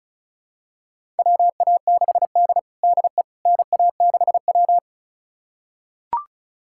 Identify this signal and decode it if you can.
Morse code, heard as a keyed tone, about 36 wpm, 700 Hz. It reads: WA6D DE NA6W
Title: USA Callsign DE Callsign - 4 Characters - 35wpm